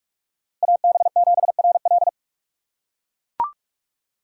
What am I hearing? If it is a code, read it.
Morse code, heard as a keyed tone, about 41 wpm, 700 Hz. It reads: AB7RL